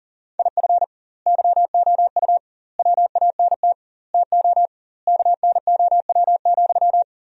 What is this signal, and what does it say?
Morse code, heard as a keyed tone, about 37 wpm, 700 Hz. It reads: IF YOU WANT TO KNOW,
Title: 2x Word-Spacing - Sentences from Top 200 Words - 40wpm